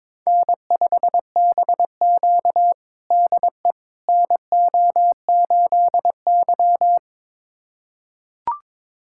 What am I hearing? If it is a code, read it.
Morse code, heard as a keyed tone, about 22 wpm, 700 Hz. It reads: N5BQ DE NO8Y
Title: USA Callsign DE Callsign - 4 Characters - 22wpm